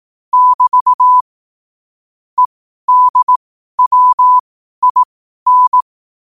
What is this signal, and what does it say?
Morse code, heard as a keyed tone, about 18 wpm, spaced as for 12 wpm, 1000 Hz. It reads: <BT> EDWIN
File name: qotdfiles / 12wpm